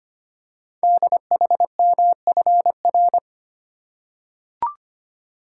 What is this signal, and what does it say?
Morse code, heard as a keyed tone, about 25 wpm, 700 Hz. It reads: DHMFR
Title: Code Groups - 5 Letters - 25wpm